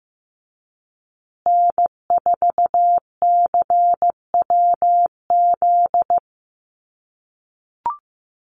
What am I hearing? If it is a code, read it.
Morse code, heard as a keyed tone, about 15 wpm, 700 Hz. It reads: N4CWZ
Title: USA Call Signs Repeated 3x 15wpm